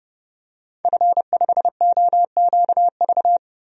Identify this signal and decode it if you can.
Morse code, heard as a keyed tone, about 29 wpm, 700 Hz. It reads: F5OQV
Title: DXCC-100 Call Signs - 5 Characters - 30wpm